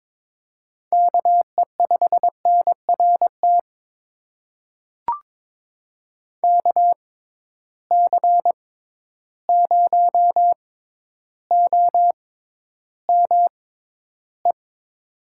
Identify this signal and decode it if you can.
Morse code, heard as a keyed tone, about 22 wpm, spaced as for 11 wpm, 700 Hz. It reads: KE5NRT KC0OME
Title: USA Call Signs - 6 Characters - 6x Character Spacing - 22wpm